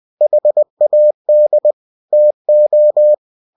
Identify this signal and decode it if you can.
Morse code, heard as a keyed tone, about 20 wpm, 600 Hz. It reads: HAD TO
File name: cw-books / winnie-the-pooh / winnie-20wpm-page-0070